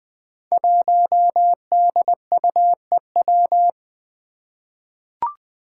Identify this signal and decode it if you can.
Morse code, heard as a keyed tone, about 20 wpm, 700 Hz. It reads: 1DUEW
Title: Five Challenging Letter-Number Combinations 20wpm